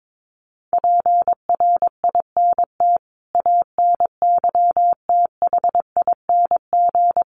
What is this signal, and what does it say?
Morse code, heard as a keyed tone, about 22 wpm, 700 Hz. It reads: PRINT ANYTHING
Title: Sentences from Top 1400 Words 22wpm